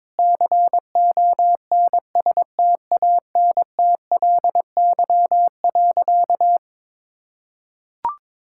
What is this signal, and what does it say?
Morse code, heard as a keyed tone, about 21 wpm, 700 Hz. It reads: CONSTANTLY.
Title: Sentences from Top 1600 Words 22wpm